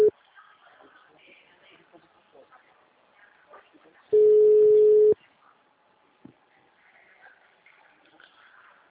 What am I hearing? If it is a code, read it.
Ringback tone (Europe, 425 Hz), with speech.
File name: out- -2003- - - .19392